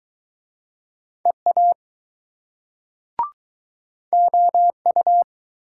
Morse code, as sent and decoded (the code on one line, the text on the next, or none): . .- / --- ..-
EA OU